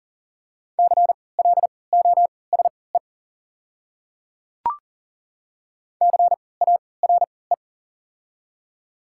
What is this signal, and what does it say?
Morse code, heard as a keyed tone, about 40 wpm, spaced as for 23 wpm, 690 Hz. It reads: CLOSE CARE
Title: 3x Character Spacing - Top 300 Words - 40wpm